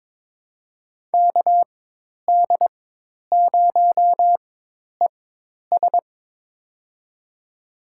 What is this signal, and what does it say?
Morse code, heard as a keyed tone, about 22 wpm, spaced as for 10 wpm, 700 Hz. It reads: KD0ES
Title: USA Call Signs - 5 Characters - 4x Character Spacing - 22wpm